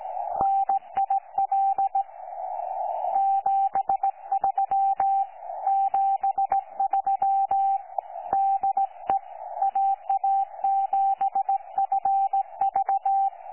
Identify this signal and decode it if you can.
Morse code, heard as a keyed tone, about 18 wpm, 780 Hz. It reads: NIL 73 73 DE AA7FV
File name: 12 45Z 18088.1kHz